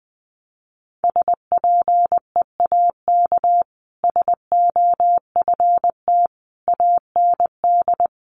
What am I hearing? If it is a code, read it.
Morse code, heard as a keyed tone, about 20 wpm, 700 Hz. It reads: SPEAK SOFT AND
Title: Sentences from Top 1500 Words 20wpm